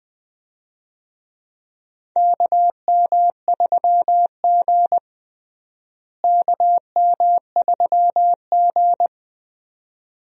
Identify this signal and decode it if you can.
Morse code, heard as a keyed tone, about 21 wpm, 700 Hz. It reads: KM3G KM3G
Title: USA Call Signs - 4 Characters - Repeated 2x - 20wpm